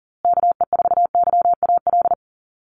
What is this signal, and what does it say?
Morse code, heard as a keyed tone, about 40 wpm, 700 Hz. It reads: KE4YAL
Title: USA Call Signs Repeated 3x 40wpm